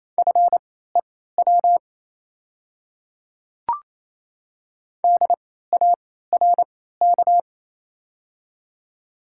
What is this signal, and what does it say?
Morse code, heard as a keyed tone, about 28 wpm, spaced as for 16 wpm, 700 Hz. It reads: FEW DARK
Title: 3x Character Spacing - Top 500 Words - 28wpm